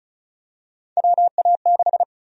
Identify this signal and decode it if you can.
Morse code, heard as a keyed tone, about 34 wpm, 700 Hz. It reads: WA6
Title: USA Call Sign Prefixes - 35wpm